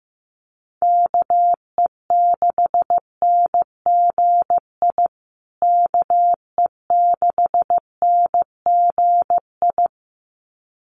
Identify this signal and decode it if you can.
Morse code, heard as a keyed tone, about 15 wpm, 700 Hz. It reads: KE6NGI KE6NGI
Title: USA Call Signs Repeated 2x 15wpm